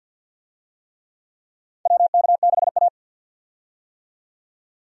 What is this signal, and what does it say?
Morse code, heard as a keyed tone, about 53 wpm, 700 Hz. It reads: WK6A
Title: USA Call Signs - 4 Characters - 50wpm